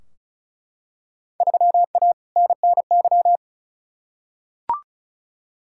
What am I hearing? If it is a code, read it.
Morse code, heard as a keyed tone, about 36 wpm, 700 Hz. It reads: 3A NNY
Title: Field Day Exchanges - 35wpm